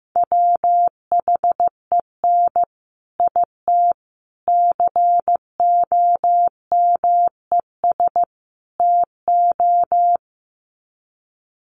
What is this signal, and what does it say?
Morse code, heard as a keyed tone, about 15 wpm, 700 Hz. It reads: WHEN IT COMES TO
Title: Sets of 4 Words 15wpm